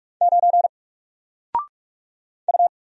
Morse code, heard as a keyed tone, about 46 wpm, 700 Hz. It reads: 9 U
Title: Single Letter-Number Alt 45wpm